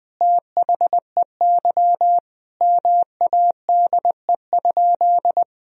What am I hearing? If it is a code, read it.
Morse code, heard as a keyed tone, about 20 wpm, 700 Hz. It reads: THEY MADE?
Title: Sentences from Top 1000 Words - Review Part 3 - 20wpm